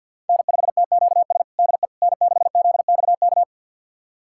Morse code, heard as a keyed tone, about 51 wpm, 700 Hz. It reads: N5TQS DE N6ZXK
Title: USA Callsign DE Callsign - 5 Characters - 50wpm